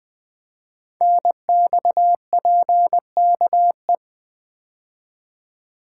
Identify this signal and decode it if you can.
Morse code, heard as a keyed tone, about 20 wpm, 700 Hz. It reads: NXPKE
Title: Code Groups - 5 Letters - 20wpm